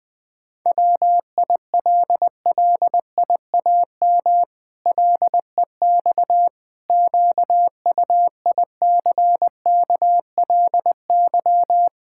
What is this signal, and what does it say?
Morse code, heard as a keyed tone, about 20 wpm, 700 Hz. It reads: WILLIAM LEX QUICKLY
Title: Pangram Challenge v2 - 20wpm